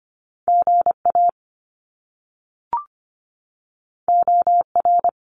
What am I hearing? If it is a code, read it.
Morse code, heard as a keyed tone, about 25 wpm, 700 Hz. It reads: GA OR
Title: Top Two Letter Combinations in Words 25wpm